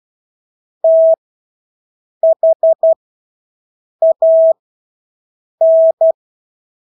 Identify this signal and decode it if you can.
Morse code, heard as a keyed tone, about 12 wpm, spaced as for 6 wpm, 650 Hz. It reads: THAN
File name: Engelske ord 12-6 wpm